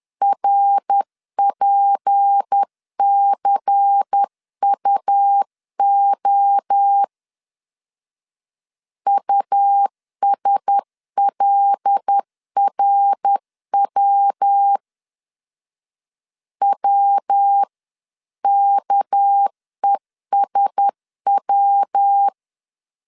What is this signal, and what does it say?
Morse code, heard as a keyed tone, about 10 wpm, 800 Hz. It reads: RPCUO USLRW WKESW
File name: CW / DET / 1 6 1500C S C 10wpm 4252